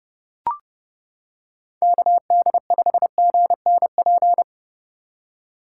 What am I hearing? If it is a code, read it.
Morse code, heard as a keyed tone, about 30 wpm, 700 Hz. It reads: KD5GNP